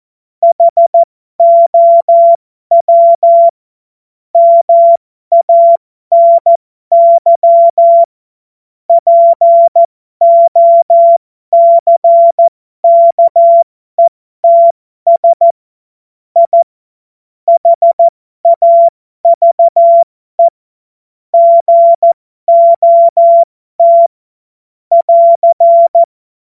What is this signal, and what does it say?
Morse code, heard as a keyed tone, about 14 wpm, spaced as for 12 wpm, 680 Hz. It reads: HOW MANY POCKETS I HAVE GOT <AR>
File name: Morsekurs / Witze